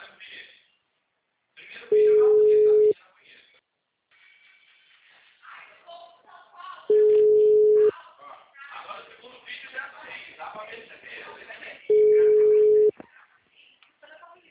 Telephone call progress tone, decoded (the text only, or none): ringback tone